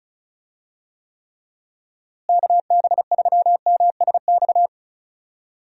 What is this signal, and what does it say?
Morse code, heard as a keyed tone, about 34 wpm, 700 Hz. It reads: KB3MSX
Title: USA Call Signs - 6 Characters - Repeated 3x - 35wpm